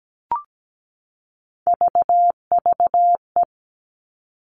Morse code, heard as a keyed tone, about 17 wpm, 700 Hz. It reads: VVE
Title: All Cut Numbers - 3-Digit 17wpm